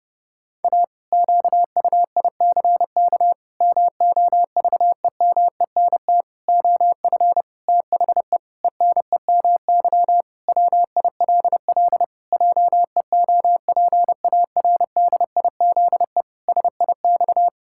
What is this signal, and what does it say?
Morse code, heard as a keyed tone, about 32 wpm, 700 Hz. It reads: A QUICK MOVEMENT OF THE ENEMY WILL JEOPARDIZE SIX
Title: Pangram Challenge v2 - 30wpm